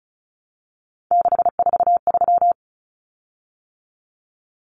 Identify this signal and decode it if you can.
Morse code, heard as a keyed tone, about 34 wpm, 700 Hz. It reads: 643
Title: Common Cut Numbers - 3-Digit 35wpm